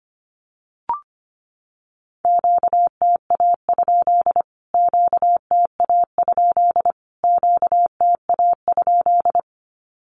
Morse code, heard as a keyed tone, about 25 wpm, 700 Hz. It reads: QTA? QTA? QTA?
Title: Q-Codes Repeated 3x 25wpm